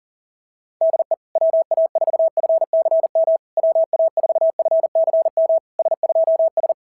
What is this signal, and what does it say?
Morse code, heard as a keyed tone, about 42 wpm, 690 Hz. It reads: DE WA4FCM WA4FCM S2S
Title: SOTA QSO Practice 40wpm